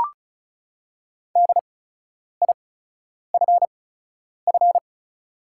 Morse code, heard as a keyed tone, about 36 wpm, spaced as for 10 wpm, 700 Hz. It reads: DIFF
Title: 8x Character Spacing - Top 300 Words - 35wpm